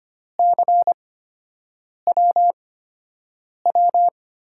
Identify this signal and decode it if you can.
Morse code, heard as a keyed tone, about 25 wpm, spaced as for 7 wpm, 700 Hz. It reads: CWW
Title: USA Call Signs - 8x Character Spacing - 25wpm